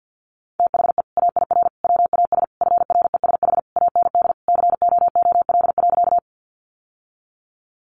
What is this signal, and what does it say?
Morse code, heard as a keyed tone, about 48 wpm, 700 Hz. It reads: THE AIR WAS FRESH AND COOL.
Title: Sentences from Top 1200 Words 50wpm